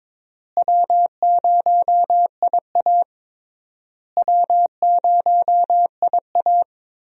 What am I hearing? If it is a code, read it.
Morse code, heard as a keyed tone, about 22 wpm, 700 Hz. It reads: W0IA W0IA
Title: USA Call Signs - 4 Characters - Repeated 3x - 22wpm